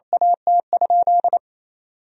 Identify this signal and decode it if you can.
Morse code, heard as a keyed tone, about 28 wpm, 700 Hz. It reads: AT?
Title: Sentences from Top 1000 Words - Review Part 3 - 28wpm